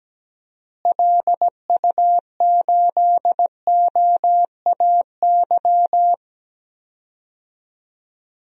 Morse code, heard as a keyed tone, about 17 wpm, 700 Hz. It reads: LU8OAY
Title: DXCC-100 Call Signs - 6 Characters - Repeated 3x - 17wpm